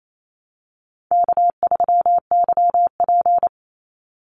Morse code, heard as a keyed tone, about 28 wpm, 700 Hz. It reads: K3YP
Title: Easy Call Signs - 28wpm